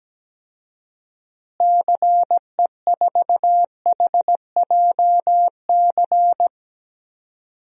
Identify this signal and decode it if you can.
Morse code, heard as a keyed tone, about 17 wpm, 700 Hz. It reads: CE4HJC
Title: DXCC-100 Call Signs - 6 Characters - Repeated 2x - 17wpm